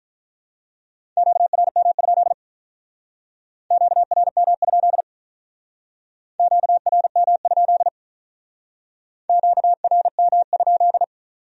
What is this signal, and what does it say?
Morse code, heard as a keyed tone, about 43 wpm, 700 Hz. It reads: QRM? QRM? QRM? QRM?
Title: Speed-Racing Q-Codes - 35wpm